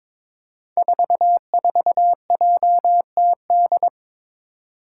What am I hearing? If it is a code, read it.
Morse code, heard as a keyed tone, about 22 wpm, 700 Hz. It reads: 44JTD
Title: Five Challenging Letter-Number Combinations 22wpm